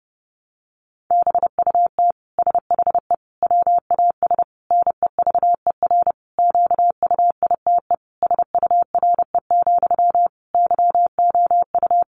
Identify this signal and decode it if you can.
Morse code, heard as a keyed tone, about 30 wpm, 700 Hz. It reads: BUT SHE WAS NEVER QUITE SURE, YOU
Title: Sentences from Top 700 Words 30wpm